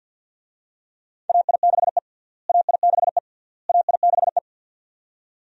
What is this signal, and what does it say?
Morse code, heard as a keyed tone, about 47 wpm, 700 Hz. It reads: AI6E AI6E AI6E
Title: USA Call Signs - 4 Characters - Repeated 3x - 50wpm